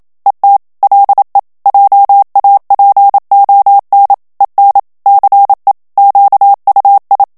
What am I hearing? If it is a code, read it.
Morse code, heard as a keyed tone, about 27 wpm, 800 Hz. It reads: ET LE JAPON EN CE QUI
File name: CW / FICHIERS-MP3 / FRANCAIS TEXTE CLAIR 1 VITESSE 26 MOTS MINUTES